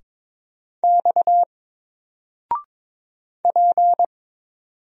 Morse code, heard as a keyed tone, about 22 wpm, 700 Hz. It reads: X P
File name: Single-Letter-Number-22wpm